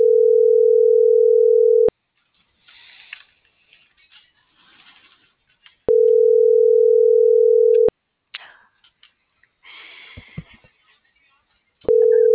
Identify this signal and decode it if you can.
Ringback tone (North America), 440 + 480 Hz.